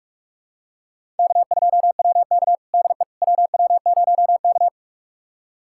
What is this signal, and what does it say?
Morse code, heard as a keyed tone, about 44 wpm, 700 Hz. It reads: K2WK DE WW0K
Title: USA Callsign DE Callsign - 4 Characters - 45wpm